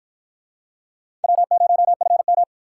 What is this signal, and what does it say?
Morse code, heard as a keyed tone, about 49 wpm, 700 Hz. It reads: W0FM